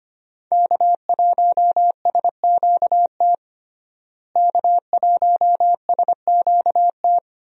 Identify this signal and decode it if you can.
Morse code, heard as a keyed tone, about 25 wpm, 700 Hz. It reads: K1SQT K1SQT
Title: USA Call Signs - 5 Characters - Repeated 3x - 25wpm